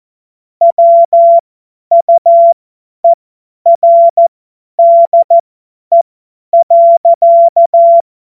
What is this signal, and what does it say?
Morse code, heard as a keyed tone, about 14 wpm, spaced as for 10 wpm, 680 Hz. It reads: WUERDE.
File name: Morsekurs / Witze / Witz 05 Dezember 2023 50 F70